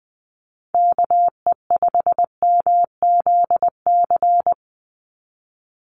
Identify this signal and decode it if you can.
Morse code, heard as a keyed tone, about 20 wpm, 700 Hz. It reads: KE5MZC
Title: USA Call Signs 20wpm